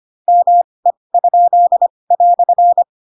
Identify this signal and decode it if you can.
Morse code, heard as a keyed tone, about 25 wpm, spaced as for 20 wpm, 700 Hz. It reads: ME?"
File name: Pinocchio / Pinocchio ch2 15 25 ik1hge